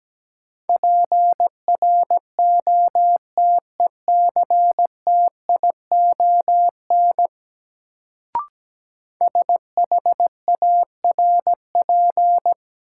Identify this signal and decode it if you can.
Morse code, heard as a keyed tone, about 17 wpm, 700 Hz. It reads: PROTECTION SHARP
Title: Sentences from Top 1600 Words - New Word Drill Alt - 17wpm